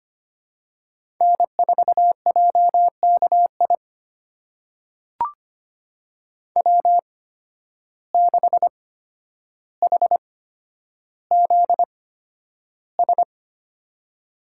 Morse code, heard as a keyed tone, about 25 wpm, spaced as for 11 wpm, 700 Hz. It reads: N4JKI W6HZS
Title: USA Call Signs - 5 Characters - 8x Character Spacing - 25wpm